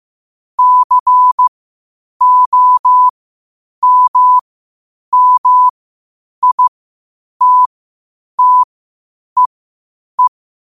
Morse code, heard as a keyed tone, about 15 wpm, spaced as for 8 wpm, 1000 Hz. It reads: COMMITTEE
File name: 8wpm